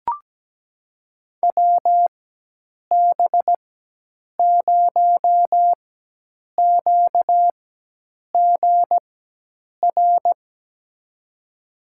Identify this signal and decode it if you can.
Morse code, heard as a keyed tone, about 17 wpm, spaced as for 8 wpm, 700 Hz. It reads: WB0QGR
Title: USA Call Signs - 6 Characters - 4x Character Spacing - 17wpm